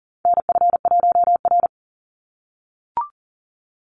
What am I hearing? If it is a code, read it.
Morse code, heard as a keyed tone, about 40 wpm, 700 Hz. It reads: NF1R
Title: NA SOTA Chaser Callsigns 40wpm